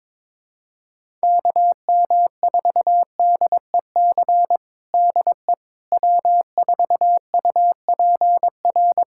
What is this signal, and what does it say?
Morse code, heard as a keyed tone, about 22 wpm, 700 Hz. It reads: KM4DEC DE W4UPR
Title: USA Callsign DE Callsign 22wpm